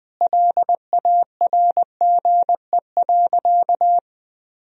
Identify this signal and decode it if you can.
Morse code, heard as a keyed tone, about 21 wpm, 700 Hz. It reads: LARGE.
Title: Sentences from Top 1000 Words - Review Part 1 - 20wpm